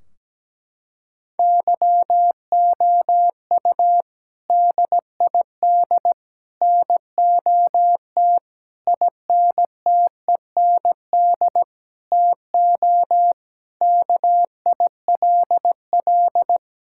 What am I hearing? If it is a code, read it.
Morse code, heard as a keyed tone, about 17 wpm, 700 Hz. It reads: YOU DID NOT INTEND TO KILL
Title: Sentences from Top 1600 Words 17wpm